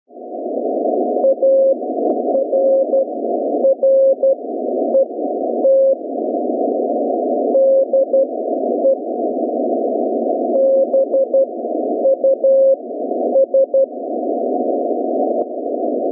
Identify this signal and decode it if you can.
Morse code, heard as a keyed tone, about 12 wpm, spaced as for 9 wpm, 540 Hz. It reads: ARRET DE BUS